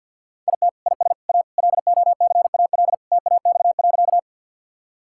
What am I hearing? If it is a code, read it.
Morse code, heard as a keyed tone, about 52 wpm, 700 Hz. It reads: IT IS A LOCAL TAX.
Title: Sentences from Top 1000 Words - Review Part 5 - 50wpm